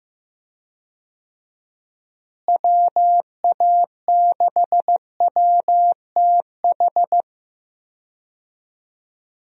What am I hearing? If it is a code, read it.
Morse code, heard as a keyed tone, about 15 wpm, 700 Hz. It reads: WA6WTH
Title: USA Call Signs - 6 Characters - Repeated 3x - 15wpm